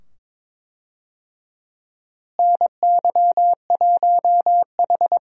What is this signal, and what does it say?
Morse code, heard as a keyed tone, about 22 wpm, 700 Hz. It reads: NY1H
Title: USA Call Signs - 4 Characters - Repeated 2x - 22wpm